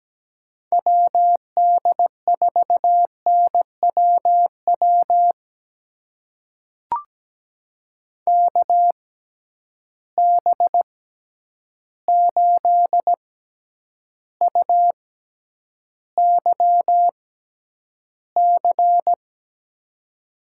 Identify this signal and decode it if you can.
Morse code, heard as a keyed tone, about 17 wpm, spaced as for 9 wpm, 700 Hz. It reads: WD4NWW KB8UYC
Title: USA Call Signs - 6 Characters - 6x Character Spacing - 17wpm